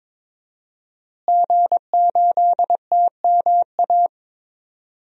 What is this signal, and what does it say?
Morse code, heard as a keyed tone, about 21 wpm, 700 Hz. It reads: G8TMA
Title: DXCC-100 Call Signs - 5 Characters - Repeated 2x - 22wpm